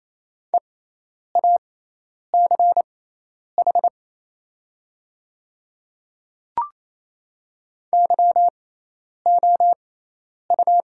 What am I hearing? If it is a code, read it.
Morse code, heard as a keyed tone, about 27 wpm, spaced as for 10 wpm, 700 Hz. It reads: EACH YOU
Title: 6x Character Spacing - Top 300 Words - 28wpm